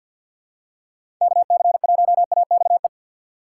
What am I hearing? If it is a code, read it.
Morse code, heard as a keyed tone, about 47 wpm, 700 Hz. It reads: KC1AXE
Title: USA Call Signs - 6 Characters - 4x Character Spacing - 50wpm